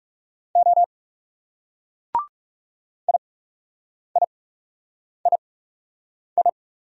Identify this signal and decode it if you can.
Morse code, heard as a keyed tone, about 38 wpm, spaced as for 8 wpm, 700 Hz. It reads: O IIII
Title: Speed-Racing Single Letter-Number 30wpm